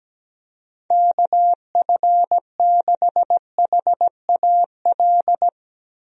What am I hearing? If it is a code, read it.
Morse code, heard as a keyed tone, about 17 wpm, 700 Hz. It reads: KF6HAL
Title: USA Call Signs - 6 Characters - 17wpm